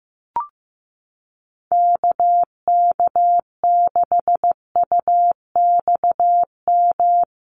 Morse code, heard as a keyed tone, about 15 wpm, 700 Hz. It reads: KK6UXM
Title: USA Call Signs Repeated 2x 15wpm